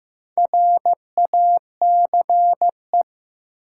Morse code, heard as a keyed tone, about 15 wpm, 700 Hz. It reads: RACE
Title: Four Letter Words - 15wpm